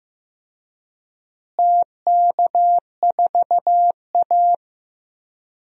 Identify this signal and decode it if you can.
Morse code, heard as a keyed tone, about 15 wpm, 700 Hz. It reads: TK4A
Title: DXCC-100 Call Signs - 4 Characters - Repeated 2x - 15wpm